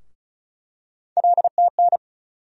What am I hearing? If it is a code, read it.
Morse code, heard as a keyed tone, about 34 wpm, 700 Hz. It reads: LTN
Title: USA Call Sign Postfixes - 3 Letters 35wpm